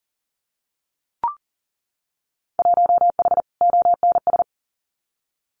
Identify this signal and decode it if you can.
Morse code, heard as a keyed tone, about 42 wpm, 700 Hz. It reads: 1H ONS
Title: Winter Field Day Exchanges 40wpm